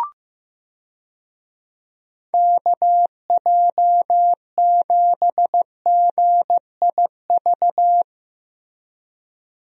Morse code, heard as a keyed tone, about 15 wpm, 700 Hz. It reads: KJ7GIV
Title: USA Call Signs - 6 Characters - Repeated 2x - 15wpm